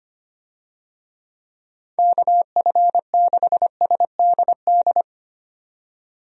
Morse code, heard as a keyed tone, about 25 wpm, 700 Hz. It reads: KF6SDD
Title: USA Call Signs - 6 Characters - 25wpm